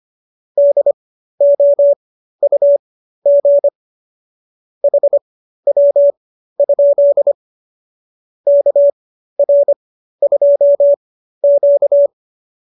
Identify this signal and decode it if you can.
Morse code, heard as a keyed tone, about 25 wpm, spaced as for 13 wpm, 570 Hz. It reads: DOUG HW? KR2Q